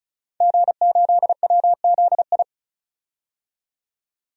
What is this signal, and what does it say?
Morse code, heard as a keyed tone, about 36 wpm, 700 Hz. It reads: G8WZI
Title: DXCC-100 Call Signs - 5 Characters - 35wpm